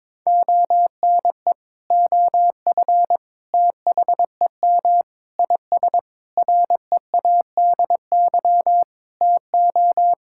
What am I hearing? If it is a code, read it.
Morse code, heard as a keyed tone, about 22 wpm, 700 Hz. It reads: ONE OF THEM IS READY TO
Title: Sentences from Top 1600 Words 22wpm